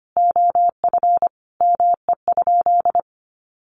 Morse code, heard as a keyed tone, about 25 wpm, 700 Hz. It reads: OF ME?
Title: Sentences from Top 600 Words 25wpm